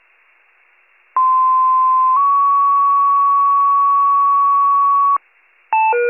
Two-tone page, heard as a one-second tone, then a three-second tone, about 1040 then 1120 Hz. It is heard from 1.2 to 5.2 s.